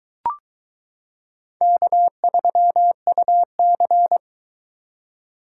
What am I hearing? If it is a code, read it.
Morse code, heard as a keyed tone, about 22 wpm, 700 Hz. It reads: K3UC